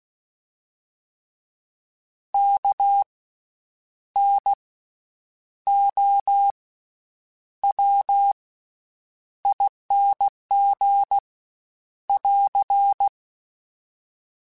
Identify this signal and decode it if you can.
Morse code, heard as a keyed tone, about 16 wpm, spaced as for 8 wpm, 800 Hz. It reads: KNOWING <AR>
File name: Special INGs-mp3s / Special INGs-mp3s / Slow ing @15